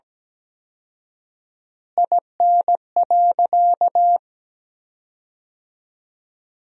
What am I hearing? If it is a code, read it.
Morse code, heard as a keyed tone, about 17 wpm, 700 Hz. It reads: IN.
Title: 4x Word-Spacing - Sentences from Top 300 Words - 17wpm